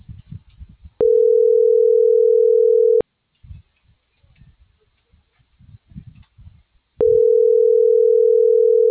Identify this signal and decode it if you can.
Ringback tone (North America), 440 + 480 Hz.